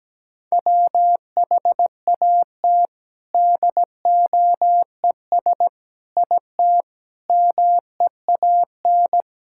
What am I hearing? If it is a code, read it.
Morse code, heard as a keyed tone, about 17 wpm, 700 Hz. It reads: WHAT DOES IT MEAN
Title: Sentences from Top 1000 Words - Review Part 6 - 17wpm